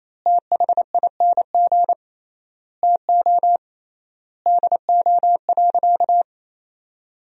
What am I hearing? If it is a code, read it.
Morse code, heard as a keyed tone, about 28 wpm, 700 Hz. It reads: THING TO DO.
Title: 3x Word-Spacing - Sentences from Top 100 Words - 28wpm